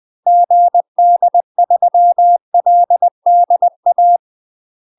Morse code, heard as a keyed tone, about 20 wpm, 700 Hz. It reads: GD3LDA
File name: UK MASTER QSOs TRAINING/20WPM